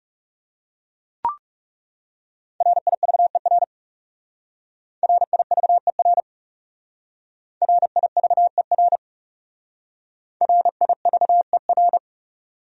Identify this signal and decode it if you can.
Morse code, heard as a keyed tone, about 35 wpm, 700 Hz. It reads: RIVER RIVER RIVER RIVER
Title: Speed-Racing Top 300 Words 30wpm